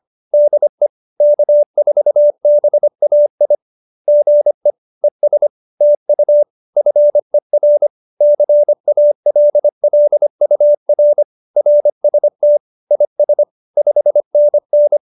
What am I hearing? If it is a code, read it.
Morse code, heard as a keyed tone, about 25 wpm, 590 Hz. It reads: DE K4BAI GE ES TU FER CALLUR RST IS 5NN
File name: QSO301 25